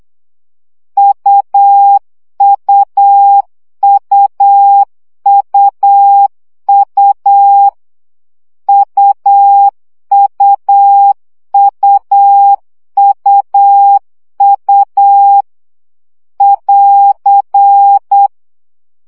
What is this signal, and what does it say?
Morse code, heard as a keyed tone, about 8 wpm, 800 Hz. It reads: UUUUU UUUUU <AR>
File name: CW gyakorlas u 7 baud